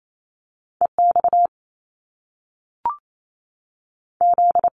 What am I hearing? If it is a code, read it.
Morse code, heard as a keyed tone, about 28 wpm, 700 Hz. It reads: E X Z
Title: Top Two Letter Combinations in Words 28wpm